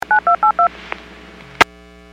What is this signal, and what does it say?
DTMF, heard as telephone touch tones, tones of about 110 ms. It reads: 9282